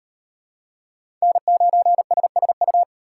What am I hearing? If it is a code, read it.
Morse code, heard as a keyed tone, about 37 wpm, 700 Hz. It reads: N9SSU